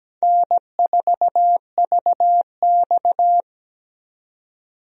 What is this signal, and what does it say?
Morse code, heard as a keyed tone, about 17 wpm, 700 Hz. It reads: N4VX
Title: USA Call Signs - 4 Characters - 17wpm